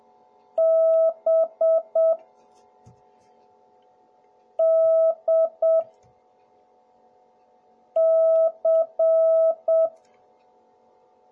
Morse code, heard as a keyed tone, about 7 wpm, 650 Hz. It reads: B D C